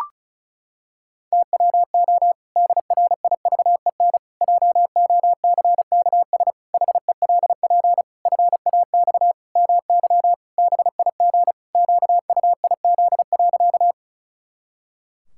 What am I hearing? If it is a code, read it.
Morse code, heard as a keyed tone, about 36 wpm, 700 Hz. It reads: TWO DRIVEN JOCKS HELP FAX MY BIG QUIZ.
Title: Pangram Challenge v2 - 35wpm